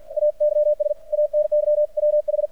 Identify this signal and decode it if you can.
Morse code, heard as a keyed tone, about 55 wpm, 600 Hz, with radio noise. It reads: AYS ANY WH
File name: archived / CRSnet